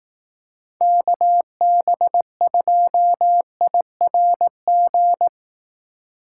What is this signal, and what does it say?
Morse code, heard as a keyed tone, about 19 wpm, 700 Hz. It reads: KB2IRG